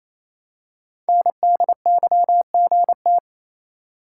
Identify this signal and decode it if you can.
Morse code, heard as a keyed tone, about 27 wpm, 700 Hz. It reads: NDYGT
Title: Code Groups - 5 Letter:Numbers - 28wpm